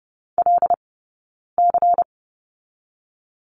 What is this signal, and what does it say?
Morse code, heard as a keyed tone, about 30 wpm, 700 Hz. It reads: L C
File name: Lesson 18d - Letter m - Two Characters 30wpm